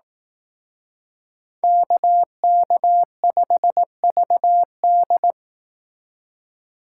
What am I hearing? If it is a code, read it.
Morse code, heard as a keyed tone, about 18 wpm, 700 Hz. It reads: KK5VD
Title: Speed Racing - USA 2x2 Call Signs 15wpm